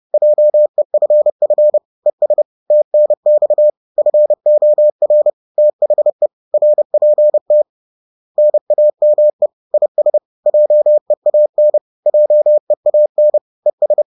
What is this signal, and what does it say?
Morse code, heard as a keyed tone, about 30 wpm, 600 Hz. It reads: JEFF ES TNX FOR THE RPT NAME IS JEAN JEAN ES